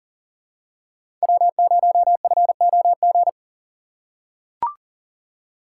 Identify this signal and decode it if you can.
Morse code, heard as a keyed tone, about 40 wpm, 700 Hz. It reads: W0FOG
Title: USA Call Signs - 5 Characters - 3x Character Spacing - 40wpm